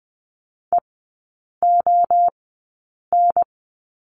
Morse code, heard as a keyed tone, about 20 wpm, spaced as for 8 wpm, 700 Hz. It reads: EON
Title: 047 Lesson 9e - Number 4 - Three Characters 20wpm